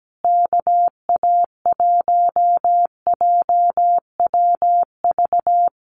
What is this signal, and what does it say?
Morse code, heard as a keyed tone, about 17 wpm, 700 Hz. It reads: KA1JWV
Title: Common QSO Phrases Encore 17wpm